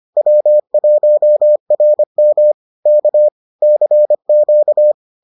Morse code, heard as a keyed tone, about 25 wpm, 600 Hz. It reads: W1RM K CQ